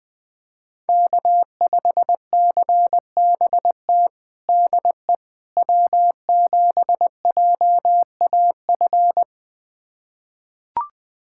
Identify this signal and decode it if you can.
Morse code, heard as a keyed tone, about 20 wpm, 700 Hz. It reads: K5CBT DE W7JAF
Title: USA Callsign DE Callsign - 5 Characters - 20wpm